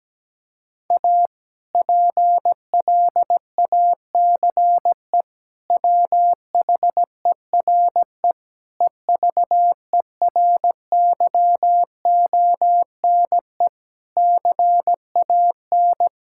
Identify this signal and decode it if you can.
Morse code, heard as a keyed tone, about 17 wpm, 700 Hz. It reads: A PLACE WHERE EVERYONE CAN
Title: Sentences from Top 1000 Words - Review Part 5 - 17wpm